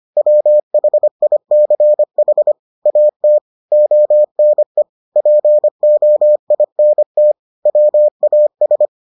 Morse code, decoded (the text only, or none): WHICH AT ONE POINT WAS